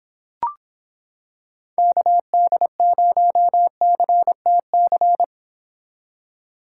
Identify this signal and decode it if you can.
Morse code, heard as a keyed tone, about 26 wpm, 700 Hz. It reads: KD0CTC